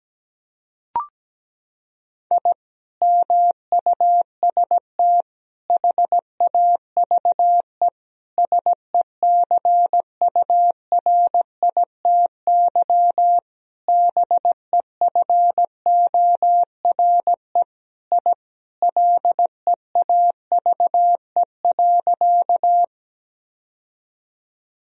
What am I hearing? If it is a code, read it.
Morse code, heard as a keyed tone, about 17 wpm, 700 Hz. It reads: I MUST HAVE SECURITY BEFORE I LEAVE.
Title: Sentences from Top 1000 Words - Review Part 4 - 17wpm